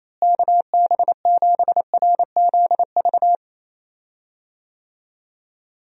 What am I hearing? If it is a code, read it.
Morse code, heard as a keyed tone, about 28 wpm, 700 Hz. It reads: KB7RZV
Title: USA Call Signs - 6 Characters - Repeated 2x - 28wpm